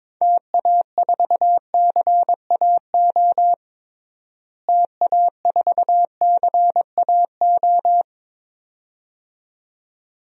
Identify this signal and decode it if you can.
Morse code, heard as a keyed tone, about 21 wpm, 700 Hz. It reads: TA4CAO TA4CAO
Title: DXCC-100 Call Signs - 6 Characters - Repeated 2x - 22wpm